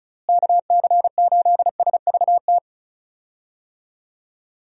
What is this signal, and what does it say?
Morse code, heard as a keyed tone, about 36 wpm, 700 Hz. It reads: KC8SVT
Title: USA Call Signs - 6 Characters - Repeated 3x - 35wpm